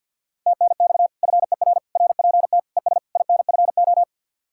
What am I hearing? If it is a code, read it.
Morse code, heard as a keyed tone, about 64 wpm, 700 Hz. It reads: TNX FER RPT ES INFO